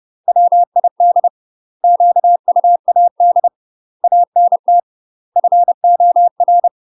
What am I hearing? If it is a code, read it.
Morse code, heard as a keyed tone, about 30 wpm, 700 Hz. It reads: WID QUAD ANT FOR
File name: UK MASTER QSOs TRAINING/30WPM